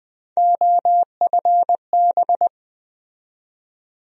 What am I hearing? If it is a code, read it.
Morse code, heard as a keyed tone, about 20 wpm, 700 Hz. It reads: OFB
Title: USA Call Sign Postfixes - 3 Letters 20wpm